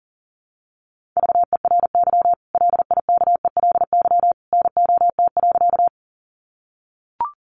Morse code, heard as a keyed tone, about 37 wpm, 700 Hz. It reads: VERY LIKELY NOT.
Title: Sentences from Top 600 Words 40wpm